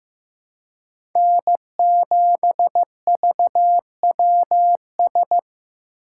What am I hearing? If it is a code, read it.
Morse code, heard as a keyed tone, about 15 wpm, 700 Hz. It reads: N7VWS
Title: USA Call Signs - 5 Characters - 3x Character Spacing - 15wpm